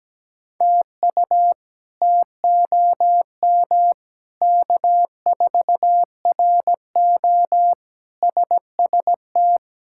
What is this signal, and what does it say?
Morse code, heard as a keyed tone, about 17 wpm, 700 Hz. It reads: TU TOM K4RO SST
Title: K1USN SST QSO Practice - 17wpm